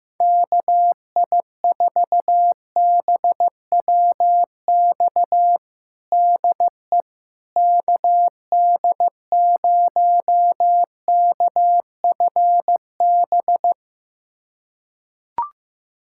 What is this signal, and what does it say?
Morse code, heard as a keyed tone, about 15 wpm, 700 Hz. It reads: KI4BWX DE KD0KFB
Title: USA Callsign DE Callsign - 6 Characters - 15wpm